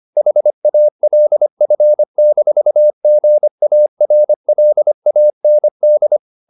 Morse code, heard as a keyed tone, about 25 wpm, 600 Hz. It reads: HALF-GARLAND
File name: the old man and the sea part 1 25wpm